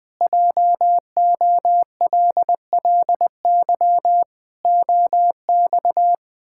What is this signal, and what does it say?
Morse code, heard as a keyed tone, about 19 wpm, 700 Hz. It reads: JOLLY OX
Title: Pangram Challenge v2 - 20wpm